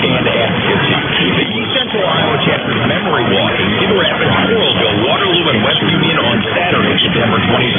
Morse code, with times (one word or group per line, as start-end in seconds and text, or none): none